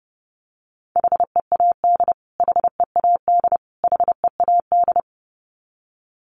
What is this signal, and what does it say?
Morse code, heard as a keyed tone, about 30 wpm, 700 Hz. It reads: HEAD HEAD HEAD
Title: Top 500 Words Repeated 30wpm